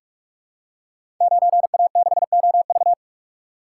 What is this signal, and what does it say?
Morse code, heard as a keyed tone, about 44 wpm, 700 Hz. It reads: 9A6OV
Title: Code Groups - 5 Letter:Numbers - 45wpm